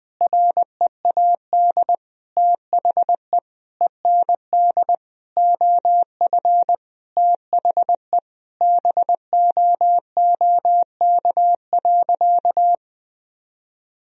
Morse code, heard as a keyed tone, about 20 wpm, 700 Hz. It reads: READ THE END OF THE BOOK.
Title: Sentences from Top 1000 Words - Review Part 1 - 20wpm